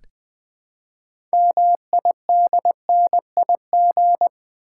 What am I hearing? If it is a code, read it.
Morse code, heard as a keyed tone, about 20 wpm, 700 Hz. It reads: MIDNIG
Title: Build A Word - 2 Letters At A Time - 8-Letter Words - 20wpm